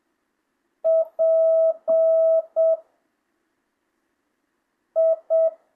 Morse code, heard as a keyed tone, about 7 wpm, 660 Hz. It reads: P I
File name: jh8cbh / 2025cw